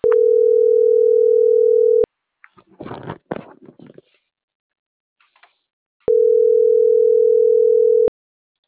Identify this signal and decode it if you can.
Ringback tone (North America), 440 + 480 Hz.